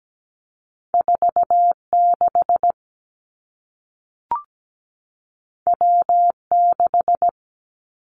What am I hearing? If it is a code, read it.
Morse code, heard as a keyed tone, about 17 wpm, 700 Hz. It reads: 46 W6
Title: Two Challenging Letter-Number Combinations 17wpm